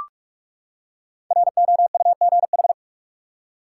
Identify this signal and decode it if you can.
Morse code, heard as a keyed tone, about 46 wpm, 700 Hz. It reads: ROUGH
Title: Five Letter Words - 45wpm